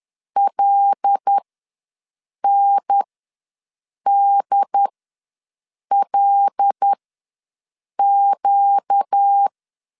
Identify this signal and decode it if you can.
Morse code, heard as a keyed tone, about 10 wpm, spaced as for 6 wpm, 800 Hz. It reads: LNDLQ
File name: LEZIONI / 5 LBVC / 5 L ws 10wpm 3636